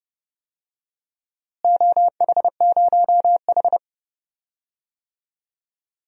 Morse code, heard as a keyed tone, about 30 wpm, 700 Hz. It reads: OH0H
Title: DXCC-100 Call Signs - 4 Characters - Repeated 2x - 30wpm